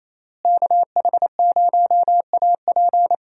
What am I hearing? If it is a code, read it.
Morse code, heard as a keyed tone, about 27 wpm, 700 Hz. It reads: KH0AP
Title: USA Call Signs - 5 Characters - Repeated 3x - 28wpm